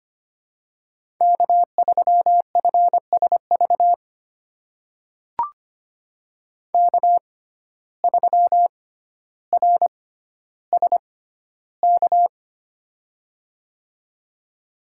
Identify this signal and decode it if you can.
Morse code, heard as a keyed tone, about 25 wpm, spaced as for 13 wpm, 700 Hz. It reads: K3FSV K3RSK
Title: USA Call Signs - 5 Characters - 6x Character Spacing - 25wpm